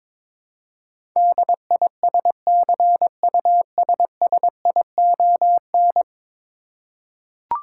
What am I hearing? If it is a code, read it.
Morse code, heard as a keyed tone, about 22 wpm, 700 Hz. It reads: DISCUSSION
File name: 10 Letter Words - First 3 Letters Given - 22wpm